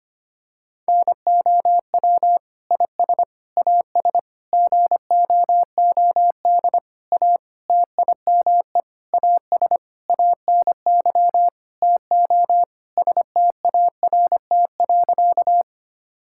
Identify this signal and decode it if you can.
Morse code, heard as a keyed tone, about 25 wpm, 700 Hz. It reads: NOW IS AS GOOD A TIME AS ANY TO START.
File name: practice / 25wpm / Sentences with 9-Word Phrases - 25wpm